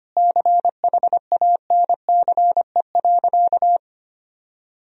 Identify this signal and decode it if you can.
Morse code, heard as a keyed tone, about 25 wpm, 700 Hz. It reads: CHANCE.
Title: Sentences from Top 1000 Words - Review Part 3 - 25wpm